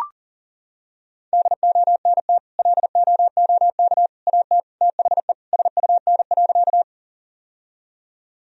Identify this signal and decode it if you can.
Morse code, heard as a keyed tone, about 40 wpm, 700 Hz. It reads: DONT LOOK AT THE SUN.
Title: Sentences from Top 1000 Words - Review Part 1 - 40wpm